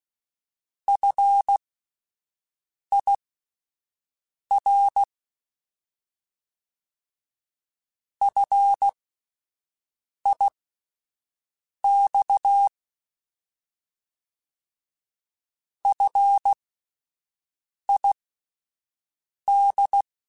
Morse code, heard as a keyed tone, about 16 wpm, spaced as for 5 wpm, 800 Hz. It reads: FIR FIX FID
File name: Alphabetical / Alphabetical mp3s / Fs @5